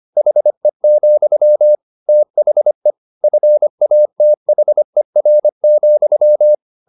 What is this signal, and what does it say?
Morse code, heard as a keyed tone, about 25 wpm, 600 Hz. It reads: HE, THE FATHER,
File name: the old man and the sea part 1 25wpm